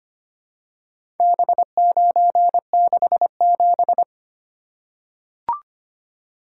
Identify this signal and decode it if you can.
Morse code, heard as a keyed tone, about 25 wpm, 690 Hz. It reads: B967
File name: practice / 25wpm / Four Letter-Number Combinations - 25wpm